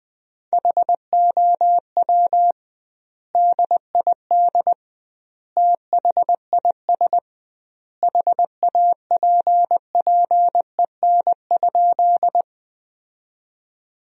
Morse code, heard as a keyed tone, about 20 wpm, 700 Hz. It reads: HOW DID THIS HAPPEN?
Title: 2x Word-Spacing - Sentences from Top 400 Words - 20wpm